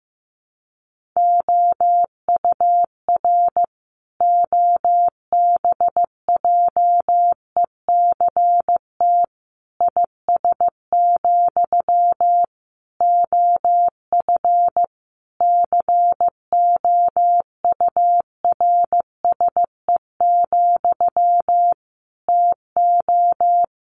Morse code, heard as a keyed tone, about 15 wpm, 700 Hz. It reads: OUR OBJECT IS, OF COURSE, TO
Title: Sentences from Top 1500 Words 15wpm